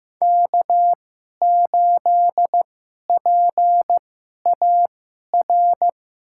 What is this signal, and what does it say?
Morse code, heard as a keyed tone, about 16 wpm, spaced as for 11 wpm, 700 Hz. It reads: K8PAR
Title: USA Call Signs - 5 Characters - 2x Character Spacing - 15wpm